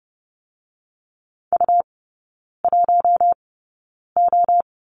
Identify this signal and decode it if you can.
Morse code, heard as a keyed tone, about 30 wpm, spaced as for 9 wpm, 690 Hz. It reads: U1O
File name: Lesson 17e - Letter u - Three Characters 30wpm